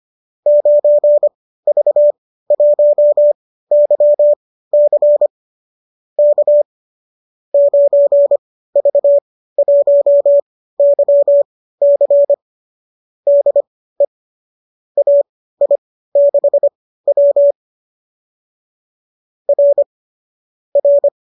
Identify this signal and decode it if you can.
Morse code, heard as a keyed tone, about 25 wpm, spaced as for 15 wpm, 570 Hz. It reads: 9V1YC K 9V1YC DE AI6W R R